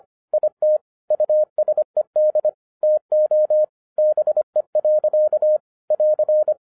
Morse code, heard as a keyed tone, about 24 wpm, 610 Hz. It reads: IT USED TO BE. <AR>